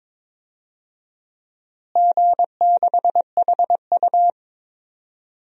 Morse code, heard as a keyed tone, about 22 wpm, 700 Hz. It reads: G6HU
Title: DXCC-100 Call Signs - 4 Characters - Repeated 3x - 22wpm